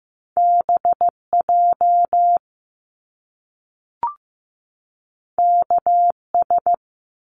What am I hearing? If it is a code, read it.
Morse code, heard as a keyed tone, about 15 wpm, 700 Hz. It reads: BJ KS
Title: Top Two Letter Combinations in Words 15wpm